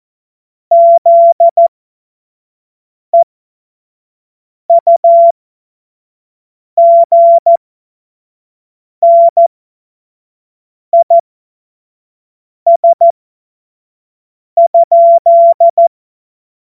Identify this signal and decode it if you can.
Morse code, heard as a keyed tone, about 14 wpm, spaced as for 5 wpm, 680 Hz. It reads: ZEUGNIS?
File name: Morsekurs / Witze / Witz 14. Februar 2023 25 F70